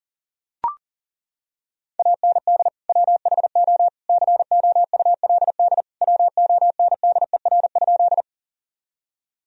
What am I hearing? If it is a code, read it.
Morse code, heard as a keyed tone, about 37 wpm, 700 Hz. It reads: AND WHO COULD WONDER?
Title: Sentences from Top 1000 Words - Review Part 3 - 40wpm